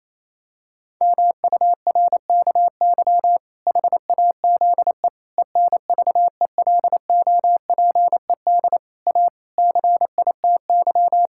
Morse code, heard as a keyed tone, about 28 wpm, 700 Hz. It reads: MURKY HAZE ENVELOPED A CITY
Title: Pangram Challenge v2 - 28wpm